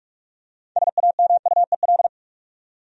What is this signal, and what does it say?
Morse code, heard as a keyed tone, about 46 wpm, 700 Hz. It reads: SAMUEL
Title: USA Common Names - 45wpm